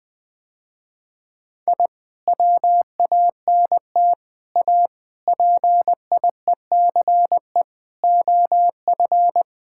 Morse code, heard as a keyed tone, about 20 wpm, 700 Hz. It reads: I WANT A PIECE OF
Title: Sentences with 3-Word Phrases - Spoken Beginning - 20wpm